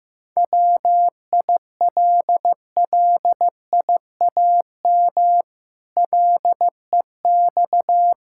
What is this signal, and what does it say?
Morse code, heard as a keyed tone, about 14 wpm, 700 Hz. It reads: WILLIAM LEX
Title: Pangram Challenge v2 - 15wpm